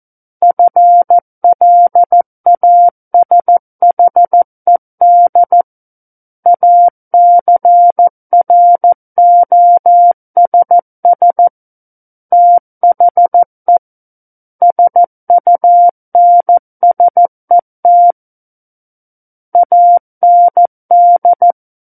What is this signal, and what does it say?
Morse code, heard as a keyed tone, about 14 wpm, 700 Hz. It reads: FLASHED ACROSS THE SUNSET AND
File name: Book One / 05-Chapter 5 - 14 WPM